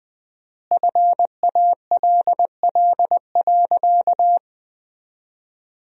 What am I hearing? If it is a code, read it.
Morse code, heard as a keyed tone, about 21 wpm, 700 Hz. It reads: FALL.
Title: 4x Word-Spacing - Sentences from Top 500 Words - 20wpm